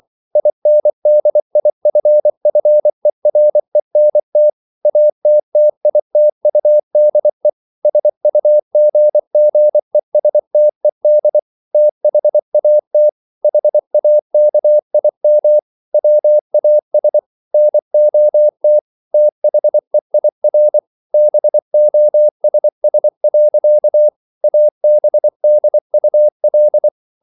Morse code, heard as a keyed tone, about 24 wpm, 600 Hz. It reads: INDIFFERENT ATTITUDE SUGGESTED THAT HAKIM WAS NOT THEIR BOSS. ABDUL